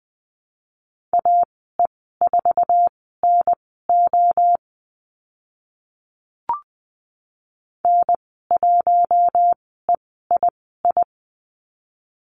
Morse code, heard as a keyed tone, about 20 wpm, spaced as for 14 wpm, 700 Hz. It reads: AE4NO N1EII
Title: 048 Lesson 9g - Number 4 - Call Signs 20wpm